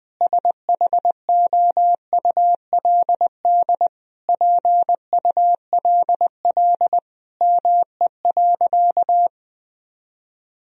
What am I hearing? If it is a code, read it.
Morse code, heard as a keyed tone, about 21 wpm, 700 Hz. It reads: SHOULD PULL ME.
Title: Sentences from Top 1000 Words - Review Part 2 - 20wpm